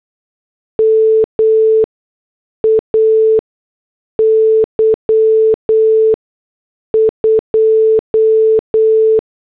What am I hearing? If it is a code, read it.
Morse code, heard as a keyed tone, about 8 wpm, spaced as for 6 wpm, 440 Hz. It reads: MAY 2